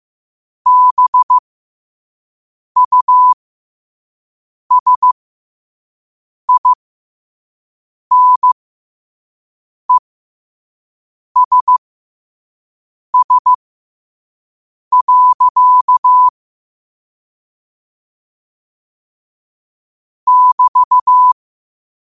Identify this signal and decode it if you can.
Morse code, heard as a keyed tone, about 15 wpm, spaced as for 5 wpm, 1000 Hz. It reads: BUSINESS. <BT>